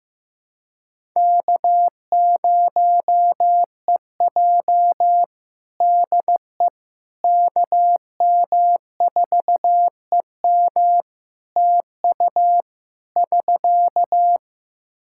Morse code, heard as a keyed tone, about 15 wpm, 700 Hz. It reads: K0EJ DE KM4EM TU <SK>
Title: Practice QSO - Exchange 4 - Ending - 15wpm